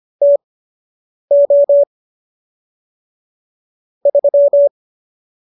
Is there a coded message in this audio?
Yes: Morse code from 0.2 to 4.7 s, heard as a keyed tone, about 25 wpm, spaced as for 8 wpm, 570 Hz. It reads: TO 3